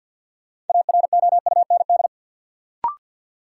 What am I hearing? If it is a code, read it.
Morse code, heard as a keyed tone, about 53 wpm, 700 Hz. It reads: AROUND
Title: Sentences from Top 600 Words - New Word Drill - 50wpm